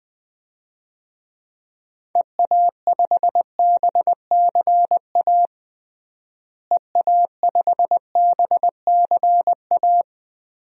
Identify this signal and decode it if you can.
Morse code, heard as a keyed tone, about 20 wpm, 700 Hz. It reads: EA5BCA EA5BCA
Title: DXCC-100 Call Signs - 6 Characters - Repeated 2x - 20wpm